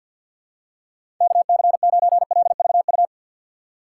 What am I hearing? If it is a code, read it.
Morse code, heard as a keyed tone, about 50 wpm, 700 Hz. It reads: KC9LVU
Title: USA Call Signs - 4x Character Spacing - 50wpm